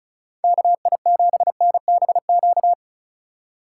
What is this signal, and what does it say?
Morse code, heard as a keyed tone, about 45 wpm, 700 Hz. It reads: KI7NBQ